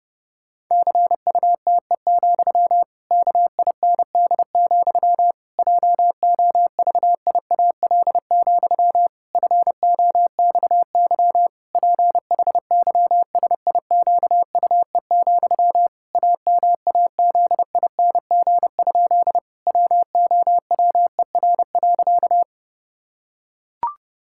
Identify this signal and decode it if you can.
Morse code, heard as a keyed tone, about 30 wpm, 700 Hz. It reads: CUTE, KIND, JOVIAL, FOXY PHYSIQUE, AMAZING? WOWER.
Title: Pangram Challenge v2 - 30wpm